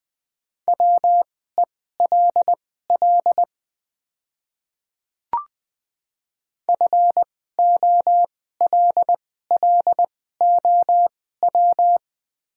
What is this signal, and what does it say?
Morse code, heard as a keyed tone, about 20 wpm, spaced as for 14 wpm, 700 Hz. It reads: WELL FOLLOW
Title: 2x Character Spacing - Top 300 Words - 20wpm